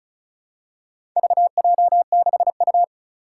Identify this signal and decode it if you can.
Morse code, heard as a keyed tone, about 36 wpm, 690 Hz. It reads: VJ6U
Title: Four Challenging Letter-Number Combinations 35wpm